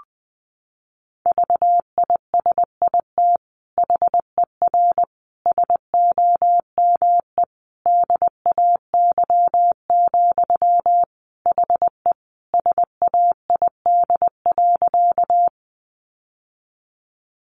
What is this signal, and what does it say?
Morse code, heard as a keyed tone, about 20 wpm, 700 Hz. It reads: VISIT HER SOME DAY, HE SAID.
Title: Sentences from Top 900 Words 20wpm